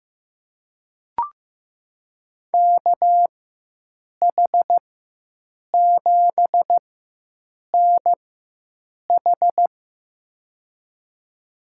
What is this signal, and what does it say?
Morse code, heard as a keyed tone, about 15 wpm, spaced as for 7 wpm, 700 Hz. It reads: KH7NH
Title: USA Call Signs - 5 Characters - 4x Character Spacing - 15wpm